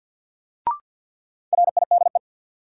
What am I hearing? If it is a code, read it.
Morse code, heard as a keyed tone, about 52 wpm, 700 Hz. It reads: RIDE
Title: Sentences from Top 1200 Words - New Word Drill Alt - 50wpm